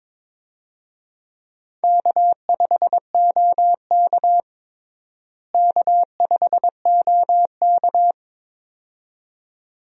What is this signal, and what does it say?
Morse code, heard as a keyed tone, about 21 wpm, 700 Hz. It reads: K5OK K5OK
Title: USA Call Signs - 4 Characters - Repeated 2x - 22wpm